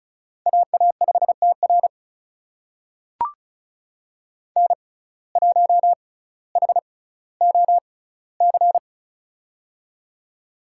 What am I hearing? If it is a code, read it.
Morse code, heard as a keyed tone, about 36 wpm, spaced as for 18 wpm, 700 Hz. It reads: AA5TR N1HOC
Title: USA Call Signs - 5 Characters - 6x Character Spacing - 35wpm